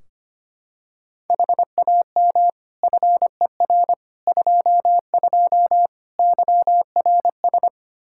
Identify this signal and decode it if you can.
Morse code, heard as a keyed tone, about 25 wpm, 700 Hz. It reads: HAM FER 22 YRS
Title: Practice QSO - Exchange 3 - More Info - 25wpm